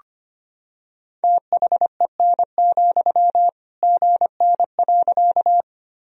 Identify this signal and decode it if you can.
Morse code, heard as a keyed tone, about 25 wpm, 700 Hz. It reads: THEN, GN.
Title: Sentences from Top 300 Words and Abbreviations 25wpm